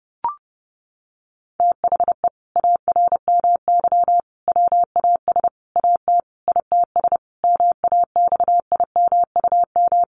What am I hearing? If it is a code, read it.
Morse code, heard as a keyed tone, about 30 wpm, 700 Hz. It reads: THE ARMY WAS AT ITS MAXIMUM
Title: Sentences from Top 1300 Words 30wpm